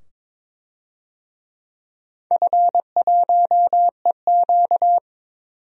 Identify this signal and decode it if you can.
Morse code, heard as a keyed tone, about 22 wpm, 700 Hz. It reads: F1EQ
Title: DXCC-100 Call Signs - 4 Characters - 22wpm